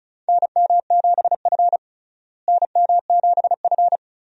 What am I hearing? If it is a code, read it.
Morse code, heard as a keyed tone, about 34 wpm, 700 Hz. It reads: NM7F NM7F
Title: USA Call Signs - 4 Characters - Repeated 3x - 35wpm